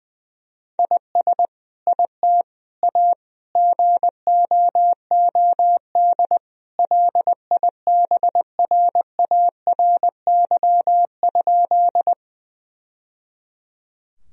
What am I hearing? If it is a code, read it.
Morse code, heard as a keyed tone, about 21 wpm, 700 Hz. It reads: IS IT A GOOD LIBRARY?
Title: Sentences from Top 1000 Words - Review Part 4 - 20wpm